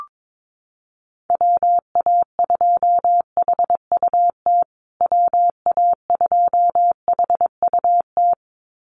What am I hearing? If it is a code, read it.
Morse code, heard as a keyed tone, about 22 wpm, 700 Hz. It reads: WA2HUT WA2HUT
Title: USA Call Signs Repeated 2x 22wpm